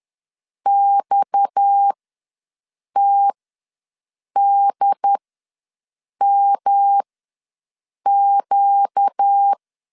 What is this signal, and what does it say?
Morse code, heard as a keyed tone, about 10 wpm, spaced as for 6 wpm, 800 Hz. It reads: XTDMQ
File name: LEZIONI / 6 PKX0 / 6 PKX ws 10wpm 3706